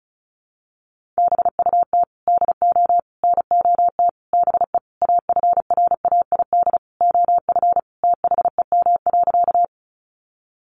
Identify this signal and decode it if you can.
Morse code, heard as a keyed tone, about 34 wpm, 700 Hz. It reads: BUT DO NOT BE AFRAID OF THEM.
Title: Sentences from Top 1500 Words 35wpm